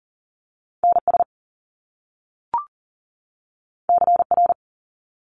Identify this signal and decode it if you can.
Morse code, heard as a keyed tone, about 38 wpm, 700 Hz. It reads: NS CR
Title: Top Two Letter Combinations in Words 40wpm